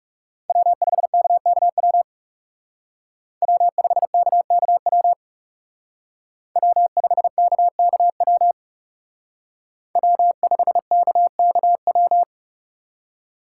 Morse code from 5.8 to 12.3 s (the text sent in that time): W5KKW W5KKW